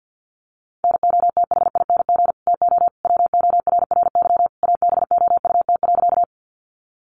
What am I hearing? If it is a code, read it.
Morse code, heard as a keyed tone, about 51 wpm, 700 Hz. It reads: NOTHING TO WORRY ABOUT.
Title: Sentences from Top 1100 Words 50wpm